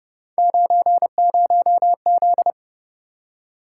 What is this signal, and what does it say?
Morse code, heard as a keyed tone, about 29 wpm, 700 Hz. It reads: 90Z